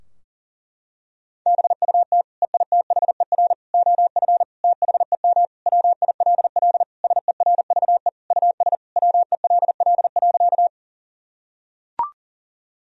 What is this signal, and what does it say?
Morse code, heard as a keyed tone, about 40 wpm, 700 Hz. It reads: BUT EITHER OF THEM WILL SERVE US WELL.
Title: Sentences from Top 1000 Words - Review Part 4 - 40wpm